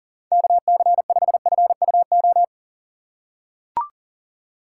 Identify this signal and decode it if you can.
Morse code, heard as a keyed tone, about 40 wpm, 700 Hz. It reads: KC5FUO
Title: USA Call Signs - 6 Characters - 3x Character Spacing - 40wpm